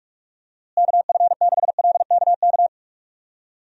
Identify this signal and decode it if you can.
Morse code, heard as a keyed tone, about 45 wpm, 700 Hz. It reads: KF6LKK
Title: USA Call Signs - 6 Characters - Repeated 2x - 45wpm